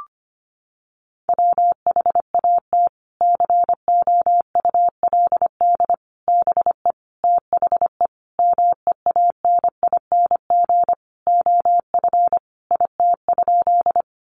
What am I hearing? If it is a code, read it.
Morse code, heard as a keyed tone, about 25 wpm, 700 Hz. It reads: WHAT COULD BE THE MEANING OF IT?
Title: Sentences from Top 900 Words 25wpm